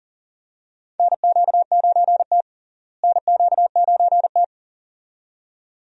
Morse code, heard as a keyed tone, about 40 wpm, 700 Hz. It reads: NQ9T NQ9T
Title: USA Call Signs - 4 Characters - Repeated 2x - 40wpm